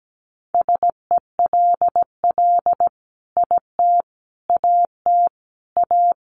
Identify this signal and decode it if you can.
Morse code, heard as a keyed tone, about 17 wpm, 700 Hz. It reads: SELL IT AT A
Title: Sentences from Top 1200 Words 17wpm